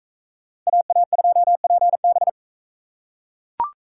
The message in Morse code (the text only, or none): AA2PB